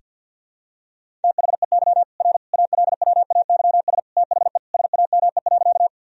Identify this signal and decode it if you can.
Morse code, heard as a keyed tone, about 50 wpm, 700 Hz. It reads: THEY R ALWAYS THE SAME.
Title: Sentences from Top 300 Words and Abbreviations 50wpm